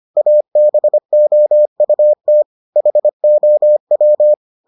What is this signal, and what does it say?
Morse code, heard as a keyed tone, about 25 wpm, 600 Hz. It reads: ABOUT HOW